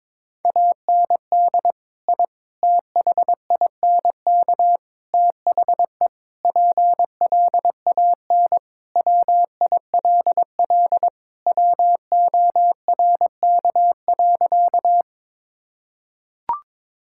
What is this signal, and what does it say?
Morse code, heard as a keyed tone, about 22 wpm, 700 Hz. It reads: AND I THINK THE PLAN WILL WORK.
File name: practice / 22wpm / Sentences with 4-Word Phrases - 22wpm